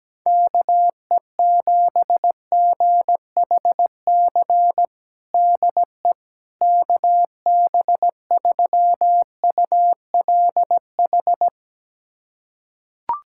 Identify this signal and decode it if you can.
Morse code, heard as a keyed tone, about 17 wpm, 700 Hz. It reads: KE7GHC DE KB3ULH
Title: USA Callsign DE Callsign - 6 Characters - 17wpm